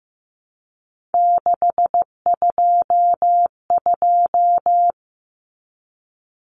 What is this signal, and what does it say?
Morse code, heard as a keyed tone, about 15 wpm, 700 Hz. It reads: 622
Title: Common Cut Numbers - 3-Digit 15wpm